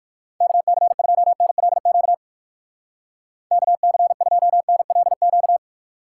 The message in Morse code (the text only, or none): KC2NLQ KC2NLQ